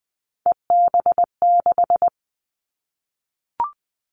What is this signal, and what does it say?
Morse code, heard as a keyed tone, about 19 wpm, 700 Hz. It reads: EB6
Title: Callsign Prefixes for Top 100 DXCC - 20wpm